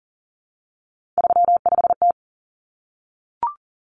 Morse code, heard as a keyed tone, about 38 wpm, 700 Hz. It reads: 35T
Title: Common Cut Numbers - 3-Digit 40wpm